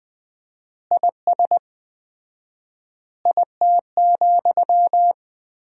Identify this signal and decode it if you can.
Morse code, heard as a keyed tone, about 20 wpm, 700 Hz. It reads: IS IT,
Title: 4x Word-Spacing - Sentences from Top 300 Words - 20wpm